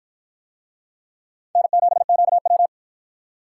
Morse code, heard as a keyed tone, about 53 wpm, 700 Hz. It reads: N7QW